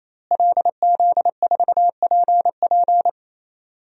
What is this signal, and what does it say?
Morse code, heard as a keyed tone, about 27 wpm, 700 Hz. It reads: LZ4PP
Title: DXCC-100 Call Signs - 5 Characters - Repeated 3x - 28wpm